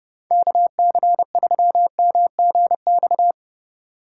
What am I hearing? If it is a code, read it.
Morse code, heard as a keyed tone, about 30 wpm, 700 Hz. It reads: KC3MGX
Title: USA Call Signs - 6 Characters - 3x Character Spacing - 30wpm